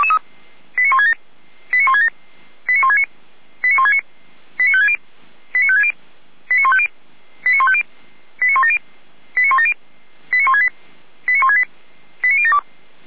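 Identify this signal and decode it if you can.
Five-tone selective call (ZVEI), heard as quick runs of five tones, tones of about 70 ms. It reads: 89167 89167 89169 89169 89560 89560 89150 89150 89170 89170 89168 89168 89072
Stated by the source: ZVEI five-tone selective call received on SDR by the fire brigade Feuerwehr Orlamünde.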